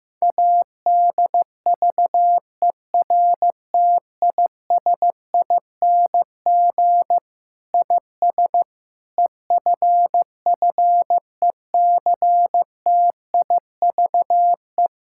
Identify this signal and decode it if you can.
Morse code, heard as a keyed tone, about 14 wpm, 700 Hz. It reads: ADVERTISING IS EFFECTIVE
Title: Sentences from Top 1000 Words - Review Part 6 - 15wpm